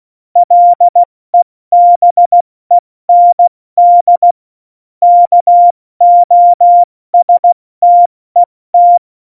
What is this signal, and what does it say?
Morse code, heard as a keyed tone, about 16 wpm, spaced as for 14 wpm, 700 Hz. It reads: LEBEND KOSTET
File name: Morsekurs / Witze / Witz 04. Dezember, 70 F80